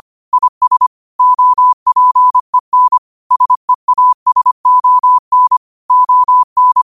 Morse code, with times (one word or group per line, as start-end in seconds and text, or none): 0.3-0.9 IS
1.2-3.0 OPEN
3.3-5.6 SEASON
5.9-6.8 ON